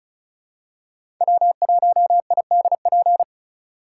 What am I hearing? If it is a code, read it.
Morse code, heard as a keyed tone, about 36 wpm, 700 Hz. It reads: W1IDP
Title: USA Call Signs - 2x Character Spacing - 35wpm